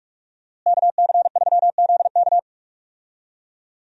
Morse code, heard as a keyed tone, about 42 wpm, 700 Hz. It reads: KC3ZK
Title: USA Call Signs - 5 Characters - Repeated 2x - 45wpm